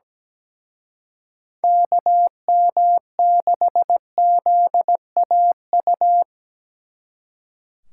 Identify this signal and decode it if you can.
Morse code, heard as a keyed tone, about 19 wpm, 700 Hz. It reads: KM6ZAU
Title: Speed Racing - USA 2x3 Call Signs 17wpm